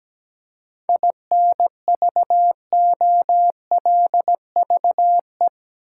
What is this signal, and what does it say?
Morse code, heard as a keyed tone, about 17 wpm, 700 Hz. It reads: INVOLVE
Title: Seven Letter Words - 17wpm